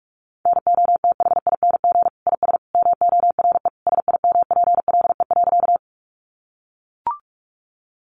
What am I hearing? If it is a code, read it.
Morse code, heard as a keyed tone, about 44 wpm, 700 Hz. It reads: NOTHING IS MORE SIMPLE.
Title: Sentences from Top 500 Words 45wpm